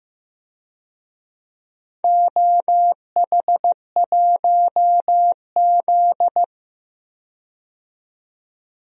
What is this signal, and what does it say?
Morse code, heard as a keyed tone, about 15 wpm, 700 Hz. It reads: OH1Z
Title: DXCC-100 Call Signs - 4 Characters - 15wpm